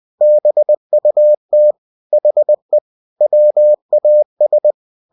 Morse code, heard as a keyed tone, about 20 wpm, 600 Hz. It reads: BUT HE WAS
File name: cw-books / winnie-the-pooh / winnie-20wpm-page-0093